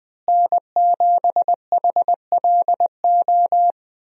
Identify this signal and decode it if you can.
Morse code, heard as a keyed tone, about 21 wpm, 700 Hz. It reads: N7HLO
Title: USA Call Signs - 5 Characters - Repeated 2x - 20wpm